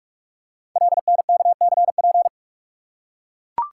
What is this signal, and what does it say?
Morse code, heard as a keyed tone, about 44 wpm, 700 Hz. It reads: LNKCP
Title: Code Groups - 5 Letter:Numbers - 45wpm